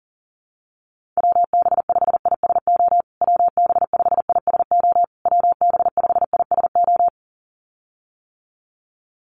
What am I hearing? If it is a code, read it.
Morse code, heard as a keyed tone, about 40 wpm, 700 Hz. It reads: WB5ISO WB5ISO WB5ISO
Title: USA Call Signs Repeated 3x 40wpm